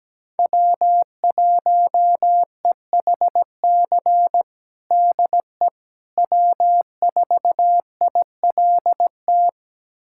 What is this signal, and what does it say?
Morse code, heard as a keyed tone, about 17 wpm, 700 Hz. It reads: W1EHC DE W4ILT
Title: USA Callsign DE Callsign - 5 Characters - 17wpm